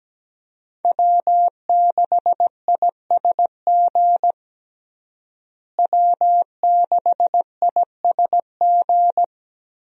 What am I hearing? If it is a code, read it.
Morse code, heard as a keyed tone, about 17 wpm, 700 Hz. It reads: W6ISG W6ISG
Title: USA Call Signs - 5 Characters - Repeated 2x - 17wpm